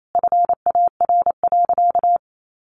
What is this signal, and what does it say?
Morse code, heard as a keyed tone, about 28 wpm, 700 Hz. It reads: FAR.
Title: Sentences from Top 400 Words 28wpm